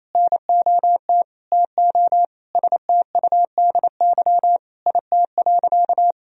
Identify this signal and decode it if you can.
Morse code, heard as a keyed tone, about 28 wpm, 700 Hz. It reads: NOT TO STUDY IT.
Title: Sentences from Top 1000 Words - Review Part 1 - 28wpm